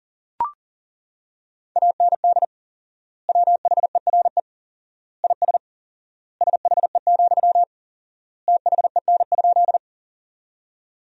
Morse code, heard as a keyed tone, about 40 wpm, 700 Hz. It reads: AND WHERE IS SHE, THEN?
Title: 4x Word-Spacing - Sentences from Top 200 Words - 40wpm